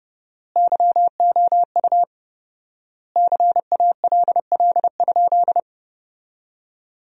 Morse code, heard as a keyed tone, about 30 wpm, 700 Hz. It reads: YOU CALL?
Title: 4x Word-Spacing - Sentences from Top 100 Words - 30wpm